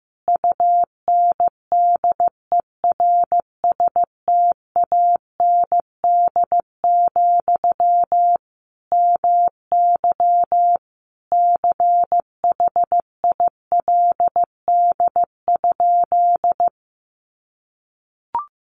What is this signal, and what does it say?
Morse code, heard as a keyed tone, about 15 wpm, 700 Hz. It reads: UNDERSTAND, MY CHILD?
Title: Sentences from Top 800 Words 15wpm